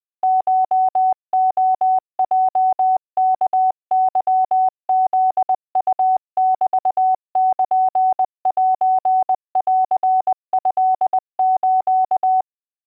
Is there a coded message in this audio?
Yes: Morse code (Japanese Wabun code) from 0.2 to 12.4 s, heard as a keyed tone, about 20 wpm, 750 Hz. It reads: コレヲワケフウメルセントス